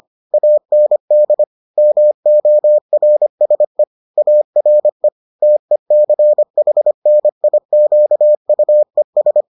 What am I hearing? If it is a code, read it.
Morse code, heard as a keyed tone, about 25 wpm, 600 Hz. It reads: AND MORSE ARE TECHNIQUES